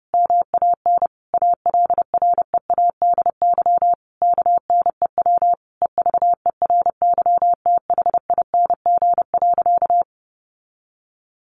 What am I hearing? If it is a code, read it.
Morse code, heard as a keyed tone, about 30 wpm, 700 Hz. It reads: MAN ALREADY KNEW EVERYTHING.
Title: Sentences from Top 700 Words 30wpm